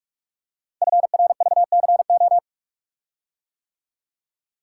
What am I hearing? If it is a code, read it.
Morse code, heard as a keyed tone, about 44 wpm, 700 Hz. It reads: FRVCO
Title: Code Groups - 5 Letters - 45wpm